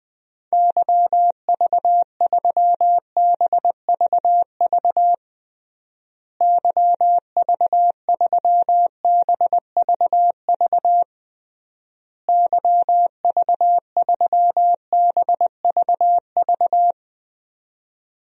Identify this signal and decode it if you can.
Morse code, heard as a keyed tone, about 20 wpm, 700 Hz. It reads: YV3BVV YV3BVV YV3BVV
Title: DXCC-100 Call Signs - 6 Characters - Repeated 3x - 20wpm